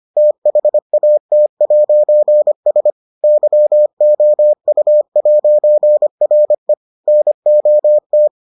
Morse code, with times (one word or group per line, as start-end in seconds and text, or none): 0.2-2.9 THAT'S
3.2-6.7 YOU'RE
7.1-8.3 NOT